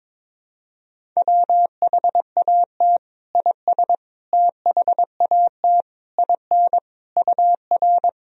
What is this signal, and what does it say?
Morse code, heard as a keyed tone, about 22 wpm, 700 Hz. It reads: WHAT IS THAT IN UR
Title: Sentences from Top 200 Words and Abbreviations 22wpm